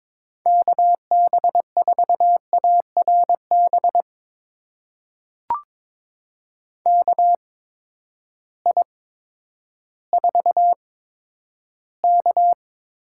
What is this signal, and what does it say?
Morse code, heard as a keyed tone, about 22 wpm, spaced as for 11 wpm, 700 Hz. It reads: KB4ARB KI4K
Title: USA Call Signs - 8x Character Spacing - 22wpm